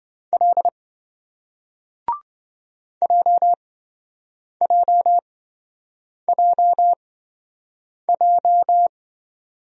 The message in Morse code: L JJJJ

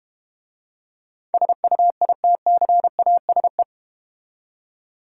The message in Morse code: SUITCASE

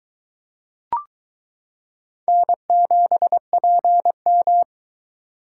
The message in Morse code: N7PM

